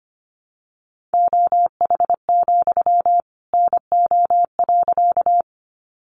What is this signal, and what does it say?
Morse code, heard as a keyed tone, about 25 wpm, 700 Hz. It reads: OH, NO.
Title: Sentences from Top 500 Words 25wpm